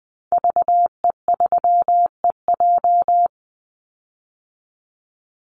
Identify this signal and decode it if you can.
Morse code, heard as a keyed tone, about 20 wpm, 700 Hz. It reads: VE3EJ
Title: Call Signs Repeated 2x - 20wpm